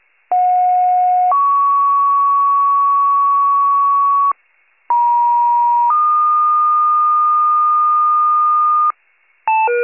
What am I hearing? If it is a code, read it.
Two-tone page, heard as a one-second tone, then a three-second tone, about 730 then 1120 Hz, about 950 then 1220 Hz. It is heard from 0.3 to 8.9 s.